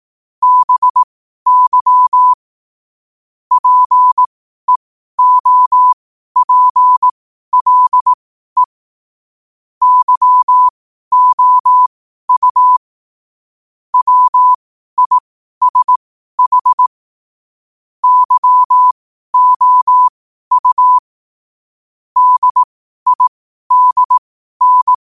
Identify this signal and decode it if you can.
Morse code, heard as a keyed tone, about 18 wpm, spaced as for 12 wpm, 1000 Hz. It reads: BY PEOPLE YOU WISH YOU DIDN
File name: qotdfiles / 12wpm